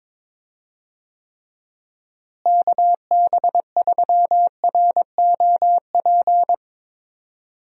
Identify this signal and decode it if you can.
Morse code, heard as a keyed tone, about 22 wpm, 700 Hz. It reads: KB3ROP
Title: USA Call Signs - 6 Characters - Repeated 3x - 22wpm